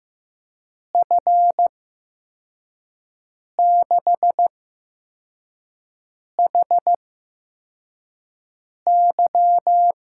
Morse code, heard as a keyed tone, about 15 wpm, spaced as for 4 wpm, 700 Hz. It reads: F6HY
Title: USA Call Signs - 5 Characters - 8x Character Spacing - 15wpm